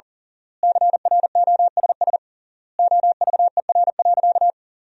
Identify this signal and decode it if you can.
Morse code, heard as a keyed tone, about 40 wpm, 700 Hz. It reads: CROSS OVER.
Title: 3x Word-Spacing - Sentences from Top 300 Words - 40wpm